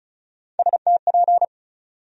Morse code, heard as a keyed tone, about 34 wpm, 700 Hz. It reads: STP
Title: USA Call Sign Postfixes - 3 Letters 35wpm